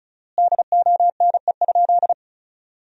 Morse code, heard as a keyed tone, about 34 wpm, 700 Hz. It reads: DONE?
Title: 4x Word-Spacing - Sentences from Top 500 Words - 35wpm